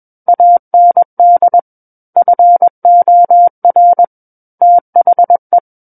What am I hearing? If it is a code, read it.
Morse code, heard as a keyed tone, about 21 wpm, 700 Hz. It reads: AND FOR THE
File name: Book One / 12-Chapter 12 - 21 WPM